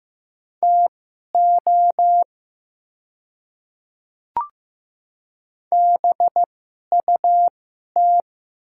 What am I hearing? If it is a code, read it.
Morse code, heard as a keyed tone, about 15 wpm, spaced as for 11 wpm, 700 Hz. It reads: TO BUT
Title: 2x Character Spacing - Top 200 Words - 15wpm